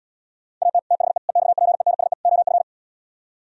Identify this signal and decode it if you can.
Morse code, heard as a keyed tone, about 77 wpm, 700 Hz. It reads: AT THE EXPENSE OF